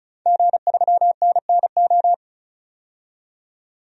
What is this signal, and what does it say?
Morse code, heard as a keyed tone, about 35 wpm, 700 Hz. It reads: G3NNO
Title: DXCC-100 Call Signs - 5 Characters - 35wpm